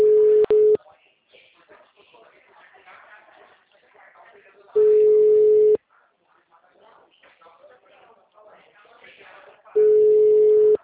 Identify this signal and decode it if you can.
Ringback tone (Europe, 425 Hz), with speech.